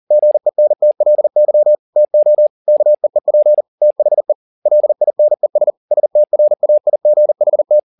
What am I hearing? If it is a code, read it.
Morse code, heard as a keyed tone, about 40 wpm, 600 Hz. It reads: GENTLY TO KEEP THE LINES STRAIGHT